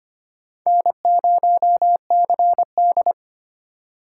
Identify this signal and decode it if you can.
Morse code, heard as a keyed tone, about 25 wpm, 700 Hz. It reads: N0CD
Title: USA Call Signs - 4 Characters - 3x Character Spacing - 25wpm